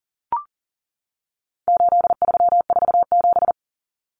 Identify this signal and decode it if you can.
Morse code, heard as a keyed tone, about 40 wpm, 700 Hz. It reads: 8347
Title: Numbers - 4-Digit 40wpm